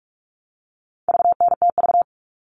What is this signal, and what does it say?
Morse code, heard as a keyed tone, about 44 wpm, 700 Hz. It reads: VNTV
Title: All Cut Numbers - 4-Digit 45wpm